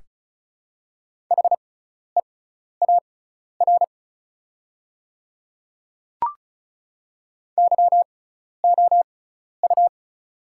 Morse code, heard as a keyed tone, about 34 wpm, spaced as for 12 wpm, 690 Hz. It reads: HEAR YOU
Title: 6x Character Spacing - Top 500 Words - 35wpm